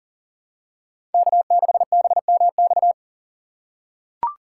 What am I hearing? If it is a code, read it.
Morse code, heard as a keyed tone, about 40 wpm, 700 Hz. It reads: K6BMX
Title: USA Call Signs - 5 Characters - 6x Character Spacing - 40wpm